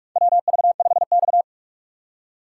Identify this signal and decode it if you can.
Morse code, heard as a keyed tone, about 47 wpm, 700 Hz. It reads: WV5X
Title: USA Call Signs - 4 Characters - 3x Character Spacing - 45wpm